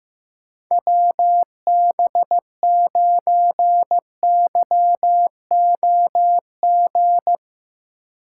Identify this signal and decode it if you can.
Morse code, heard as a keyed tone, about 15 wpm, 700 Hz. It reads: WB9YOG
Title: USA Call Signs - 6 Characters - 4x Character Spacing - 15wpm